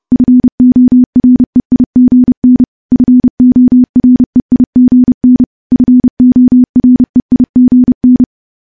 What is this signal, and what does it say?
Morse code, heard as a keyed tone, about 30 wpm, 250 Hz. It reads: FOREIGN FOREIGN FOREIGN
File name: morse / morse-training-30WPM